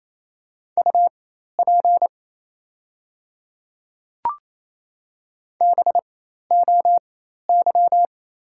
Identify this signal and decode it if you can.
Morse code, heard as a keyed tone, about 28 wpm, spaced as for 13 wpm, 700 Hz. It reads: UP BOY
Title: 4x Character Spacing - Top 200 Words - 28wpm